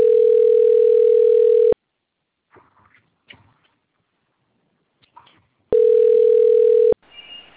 Ringback tone (North America), 440 + 480 Hz.
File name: out- - - - - .48564